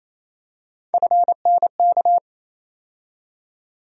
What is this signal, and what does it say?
Morse code, heard as a keyed tone, about 28 wpm, 700 Hz. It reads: FNK
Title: USA Call Sign Postfixes - 3 Letters 28wpm